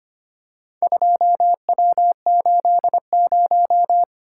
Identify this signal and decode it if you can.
Morse code, heard as a keyed tone, about 25 wpm, 700 Hz. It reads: 2W80
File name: practice / 25wpm / Four Letter-Number Combinations - 25wpm